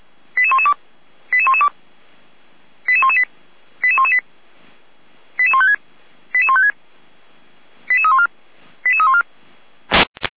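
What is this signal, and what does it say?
Five-tone selective call (ZVEI), heard as quick runs of five tones, tones of about 70 ms. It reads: 88112 88112 88118 88118 88156 88156 88314 88314
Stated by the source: ZVEI five-tone selective call received by SDR.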